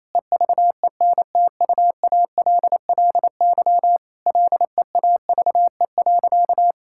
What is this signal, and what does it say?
Morse code, heard as a keyed tone, about 27 wpm, 700 Hz. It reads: EVENTUALLY LEAVE.
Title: Sentences from Top 1000 Words - Review Part 6 - 28wpm